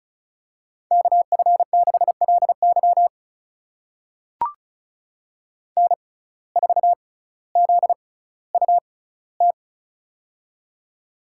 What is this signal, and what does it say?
Morse code, heard as a keyed tone, about 36 wpm, spaced as for 18 wpm, 700 Hz. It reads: KF6LY N4ZUT
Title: USA Call Signs - 5 Characters - 6x Character Spacing - 35wpm